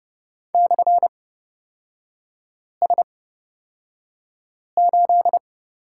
Morse code, heard as a keyed tone, about 31 wpm, spaced as for 5 wpm, 690 Hz. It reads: /S8
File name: Lesson 38e - Letter q - Three Characters 30wpm